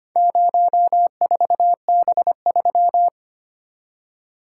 Morse code, heard as a keyed tone, about 25 wpm, 700 Hz. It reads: 04B3
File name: practice / 25wpm / Four Letter-Number Combinations - 25wpm